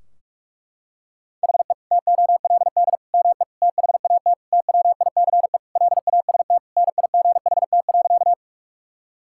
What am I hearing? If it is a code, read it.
Morse code, heard as a keyed tone, about 46 wpm, 700 Hz. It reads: HE TOLD ME THAT TWICE LAST NIGHT.
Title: Sentences from Top 1600 Words 45wpm